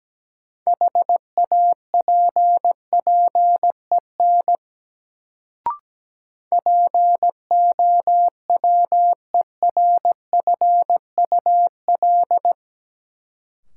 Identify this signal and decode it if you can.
Morse code, heard as a keyed tone, about 17 wpm, 700 Hz. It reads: HAPPEN POWERFUL
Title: Sets of 5 Words Encore - New Word Drill - 17wpm